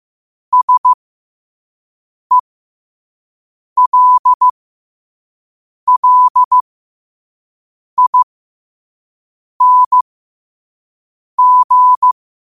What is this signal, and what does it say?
Morse code, heard as a keyed tone, about 15 wpm, spaced as for 5 wpm, 1000 Hz. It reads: SELLING
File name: qotdfiles / 5wpm